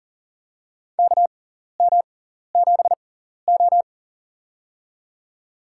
Morse code, heard as a keyed tone, about 40 wpm, spaced as for 14 wpm, 700 Hz. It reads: KM7O
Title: USA Call Signs - 4 Characters - 6x Character Spacing - 40wpm